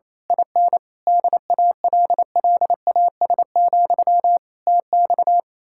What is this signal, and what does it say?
Morse code, heard as a keyed tone, about 27 wpm, 700 Hz. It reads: IN DALLAS, TX
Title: Speed-Racing Common QSO Phrases - 28wpm